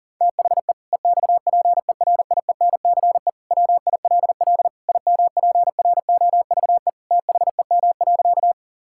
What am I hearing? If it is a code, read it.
Morse code, heard as a keyed tone, about 40 wpm, 700 Hz. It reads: THE EXPERIENCE WILL IMPROVE THEM.
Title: Sentences from Top 1000 Words - Review Part 5 - 40wpm